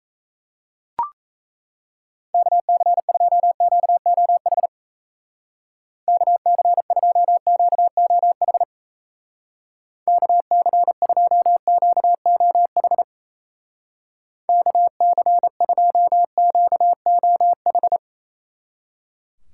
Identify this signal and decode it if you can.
Morse code, heard as a keyed tone, about 37 wpm, 700 Hz. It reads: KC2QOH KC2QOH KC2QOH KC2QOH